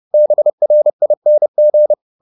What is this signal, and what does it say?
Morse code, heard as a keyed tone, about 30 wpm, 600 Hz. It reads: BRING